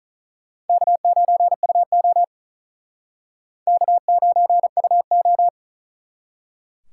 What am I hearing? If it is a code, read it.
Morse code, heard as a keyed tone, about 45 wpm, 700 Hz. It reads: K9UO K9UO